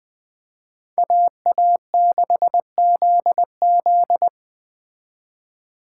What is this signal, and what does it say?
Morse code, heard as a keyed tone, about 20 wpm, 700 Hz. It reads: AA6ZZ
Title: Speed Racing - USA 2x2 Call Signs 17wpm